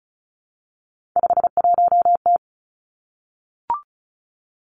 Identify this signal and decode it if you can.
Morse code, heard as a keyed tone, about 36 wpm, 700 Hz. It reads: 51T
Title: Three Challenging Letter-Number Combinations 35wpm